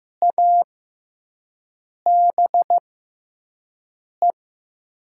Morse code, heard as a keyed tone, about 15 wpm, spaced as for 5 wpm, 700 Hz. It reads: ABE
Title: USA Call Signs - 6x Character Spacing - 15wpm